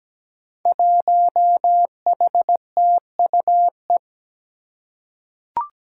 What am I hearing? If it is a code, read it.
Morse code, heard as a keyed tone, about 17 wpm, 700 Hz. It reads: 1HTUE
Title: Five Challenging Letter-Number Combinations 17wpm